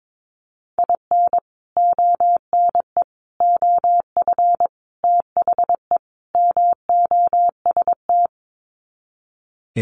Morse code, heard as a keyed tone, about 22 wpm, 700 Hz. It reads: IN ONE OF THE MOST
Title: Sets of 5 Words Encore 22wpm